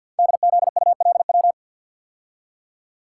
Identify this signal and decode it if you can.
Morse code, heard as a keyed tone, about 48 wpm, 700 Hz. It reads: DZULW
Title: Code Groups - 5 Letters - 50wpm